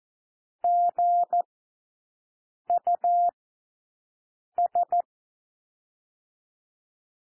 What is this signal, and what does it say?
Morse code, heard as a keyed tone, about 14 wpm, 700 Hz. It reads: G U S
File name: cwmp3 / W1 SOUND WORDS 14 wpm 0